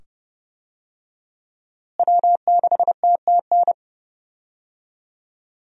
Morse code, heard as a keyed tone, about 30 wpm, 700 Hz. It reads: W6TTN
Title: USA Call Signs - 5 Characters - Repeated 2x - 30wpm